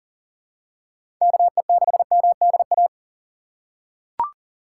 Morse code, heard as a keyed tone, about 40 wpm, 700 Hz. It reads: KE6MDA
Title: USA Call Signs - 2x Character Spacing - 40wpm